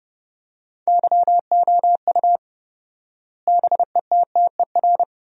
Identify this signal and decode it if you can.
Morse code, heard as a keyed tone, about 30 wpm, 700 Hz. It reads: YOU BETTER
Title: 4x Word-Spacing - Sentences from Top 500 Words - 30wpm